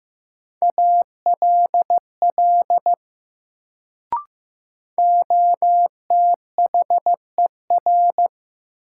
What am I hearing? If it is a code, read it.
Morse code, heard as a keyed tone, about 15 wpm, 700 Hz. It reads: ALL OTHER
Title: Sentences from Top 100 Words - New Word Drill - 15wpm